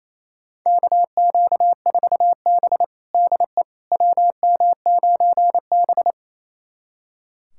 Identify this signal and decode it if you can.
Morse code, heard as a keyed tone, about 28 wpm, 700 Hz. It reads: KQ4B DE WM9B
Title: USA Callsign DE Callsign - 4 Characters - 28wpm